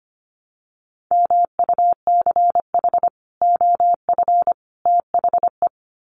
Morse code, heard as a keyed tone, about 25 wpm, 700 Hz. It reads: MUCH OF THE
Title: Sets of 3 Words 25wpm